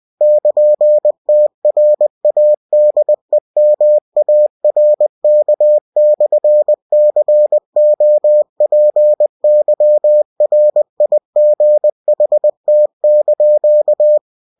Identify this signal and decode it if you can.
Morse code, heard as a keyed tone, about 20 wpm, 600 Hz. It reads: <KN>TRADEMARK/COPYRIGHT)
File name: cw-books / winnie-the-pooh / winnie-20wpm-page-0113